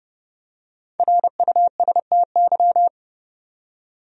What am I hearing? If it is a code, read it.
Morse code, heard as a keyed tone, about 30 wpm, 700 Hz. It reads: RUSTY
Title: USA Common Names - Max 5 Letters - 30wpm